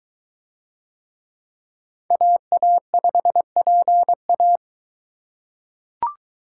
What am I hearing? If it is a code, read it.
Morse code, heard as a keyed tone, about 22 wpm, 700 Hz. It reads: AA5PA